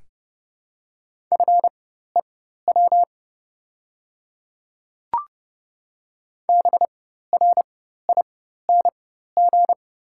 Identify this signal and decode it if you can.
Morse code, heard as a keyed tone, about 30 wpm, spaced as for 14 wpm, 690 Hz. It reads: FEW BRING
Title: 4x Character Spacing - Top 500 Words - 30wpm